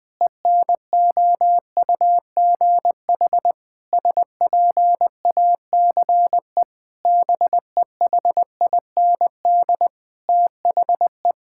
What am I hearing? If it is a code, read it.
Morse code, heard as a keyed tone, about 21 wpm, 700 Hz. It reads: ENOUGH SPACE BEHIND THE
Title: Sentences from Top 1000 Words - Review Part 5 - 20wpm